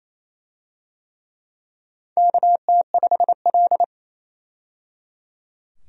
Morse code, heard as a keyed tone, about 27 wpm, 700 Hz. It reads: KT5L
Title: USA Call Signs - 4 Characters - 28wpm